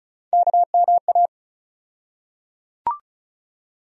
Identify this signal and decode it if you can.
Morse code, heard as a keyed tone, about 34 wpm, 700 Hz. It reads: KMA
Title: USA Call Sign Postfixes - 3 Letters 35wpm